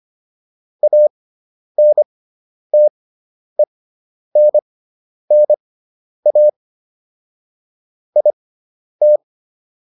Morse code, heard as a keyed tone, about 25 wpm, spaced as for 10 wpm, 600 Hz. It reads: ANTENNA IT